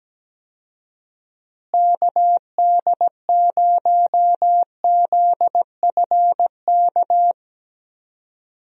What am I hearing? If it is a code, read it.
Morse code, heard as a keyed tone, about 17 wpm, 700 Hz. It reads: KD0ZFK
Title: USA Call Signs - 6 Characters - Repeated 3x - 17wpm